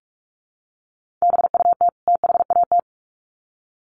Morse code, heard as a keyed tone, about 44 wpm, 700 Hz. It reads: BUT THAT
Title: Sets of 2 Words 45wpm